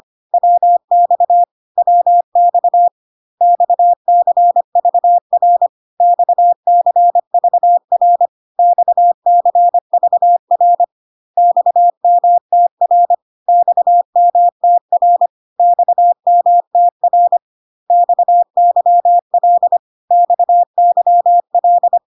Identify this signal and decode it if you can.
Morse code, heard as a keyed tone, about 25 wpm, 700 Hz. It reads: WX WX XCVR XCVR XCVR XMTR XMTR XMTR XYL XYL
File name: CW abbreviations alphabetical repeated 3 times 25 25wpm ik1hge